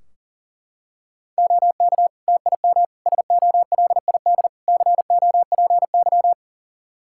Morse code, heard as a keyed tone, about 40 wpm, 700 Hz. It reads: OK TIM SOLID COPY
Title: Practice QSO - Exchange 3 - More Info - 40wpm